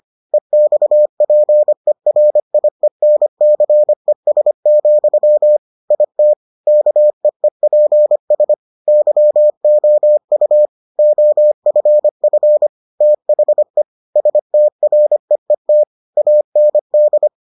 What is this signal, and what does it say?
Morse code, heard as a keyed tone, about 25 wpm, 600 Hz. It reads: EXPERIENCES, IT KEEPS YOU OFF THE STREET AND